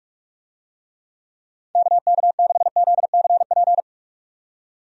Morse code, heard as a keyed tone, about 44 wpm, 700 Hz. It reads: KK6ZCP
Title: USA Call Signs - 6 Characters - 45wpm